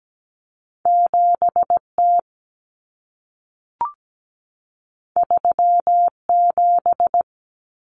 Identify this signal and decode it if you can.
Morse code, heard as a keyed tone, about 17 wpm, 700 Hz. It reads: 7T 37
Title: Common Cut Numbers - 2-Digit 17wpm